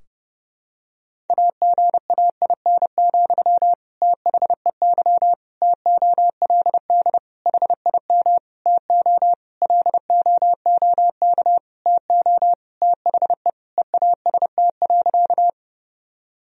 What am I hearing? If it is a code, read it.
Morse code, heard as a keyed tone, about 30 wpm, 700 Hz. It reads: AGAIN, THEY TOLD HIM TO LOOK TO THE EAST.
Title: Sentences from Top 1000 Words - Review Part 3 - 30wpm